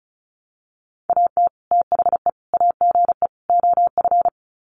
Morse code, heard as a keyed tone, about 34 wpm, 700 Hz. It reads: AT THE AGE OF
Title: Sets of 4 Words 35wpm